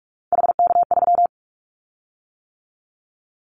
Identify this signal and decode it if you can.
Morse code, heard as a keyed tone, about 44 wpm, 700 Hz. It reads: HK3
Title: Callsign Prefixes for Top 100 DXCC - 45wpm